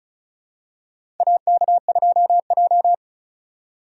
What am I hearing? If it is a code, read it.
Morse code, heard as a keyed tone, about 36 wpm, 700 Hz. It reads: AK2J
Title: USA Call Signs - 4 Characters - 3x Character Spacing - 35wpm